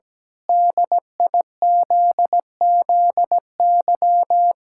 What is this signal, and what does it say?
Morse code, heard as a keyed tone, about 17 wpm, 700 Hz. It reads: DIZZY
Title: Pangram Challenge v2 - 17wpm